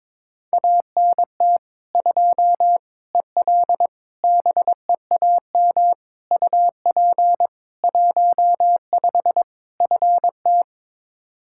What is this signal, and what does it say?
Morse code, heard as a keyed tone, about 22 wpm, 700 Hz. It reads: ANT 2 EL BEAM UP 15 FT
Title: Practice QSO - Exchange 2 - Info - 22wpm